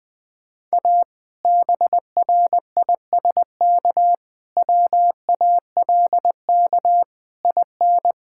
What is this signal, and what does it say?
Morse code, heard as a keyed tone, about 19 wpm, 700 Hz. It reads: A BRISK WALK IN
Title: Difficult Sentences with Hints - 20wpm